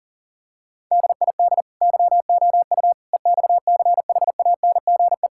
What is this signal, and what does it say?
Morse code, heard as a keyed tone, about 40 wpm, 700 Hz. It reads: DID YOU EXCHANGE
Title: Sentences from Top 1000 Words - Review Part 6 - 40wpm